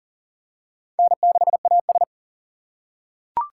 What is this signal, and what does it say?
Morse code, heard as a keyed tone, about 37 wpm, 700 Hz. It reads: N6AS
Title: USA Call Signs - 4 Characters - 6x Character Spacing - 40wpm